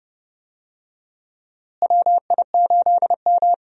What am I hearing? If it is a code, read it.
Morse code, heard as a keyed tone, about 30 wpm, 700 Hz. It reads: WI8M
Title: USA Call Signs - 4 Characters - Repeated 2x - 30wpm